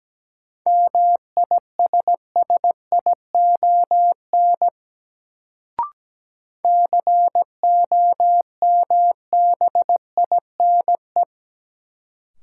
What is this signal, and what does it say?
Morse code, heard as a keyed tone, about 17 wpm, 700 Hz. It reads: MISSION COMBINE
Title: Sentences from Top 1600 Words - New Word Drill - 17wpm